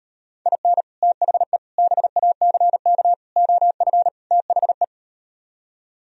Morse code, heard as a keyed tone, about 38 wpm, 700 Hz. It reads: IN THE BACK OF THE